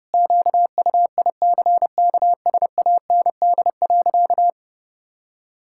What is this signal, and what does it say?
Morse code, heard as a keyed tone, about 30 wpm, 700 Hz. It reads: QUICKSAND.
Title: Pangram Challenge v2 - 30wpm